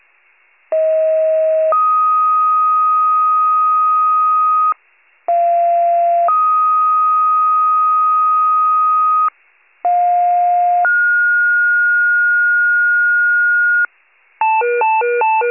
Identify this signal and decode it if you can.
Two-tone page, heard as a one-second tone, then a three-second tone, about 640 then 1250 Hz, about 690 then 1250 Hz, about 710 then 1430 Hz. It is heard from 0.7 to 13.9 s.
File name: wcalert